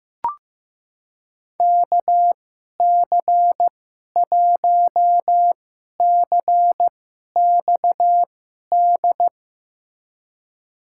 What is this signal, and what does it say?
Morse code, heard as a keyed tone, about 15 wpm, spaced as for 11 wpm, 700 Hz. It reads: KC1CXD
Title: USA Call Signs - 6 Characters - 2x Character Spacing - 15wpm